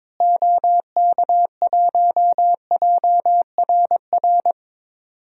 Morse code, heard as a keyed tone, about 22 wpm, 700 Hz. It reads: OK1JRR
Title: DXCC-100 Call Signs - 6 Characters - Repeated 3x - 22wpm